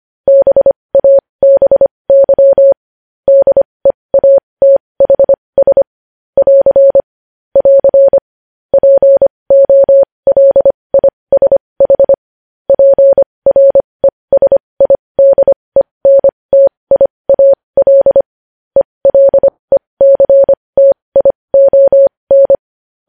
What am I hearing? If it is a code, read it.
Morse code, heard as a keyed tone, about 25 wpm, spaced as for 20 wpm, 550 Hz. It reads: BABY DEATHS <AR> <AR> POLISH PRESIDENTIAL ELECTION
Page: news-headlines-in-morse-code-at-20-wpm